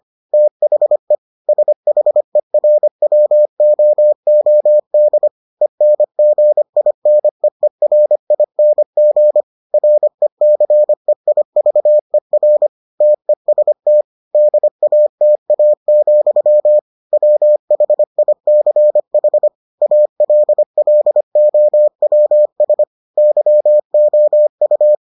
Morse code, heard as a keyed tone, about 25 wpm, 600 Hz. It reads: THE SHERWOOD ENGINEERING RECEIVER TEST DATA, WHICH ALLOWS YOU